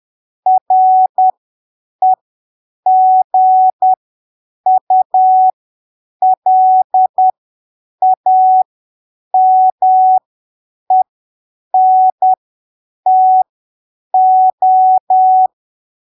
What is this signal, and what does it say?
Morse code, heard as a keyed tone, about 10 wpm, spaced as for 7 wpm, 750 Hz. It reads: REGULAMENTO
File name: arquivos / texto / texto 12 10ppm